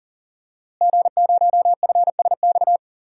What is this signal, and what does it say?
Morse code, heard as a keyed tone, about 40 wpm, 700 Hz. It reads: G0FSX
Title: DXCC-100 Call Signs - 5 Characters - Repeated 3x - 40wpm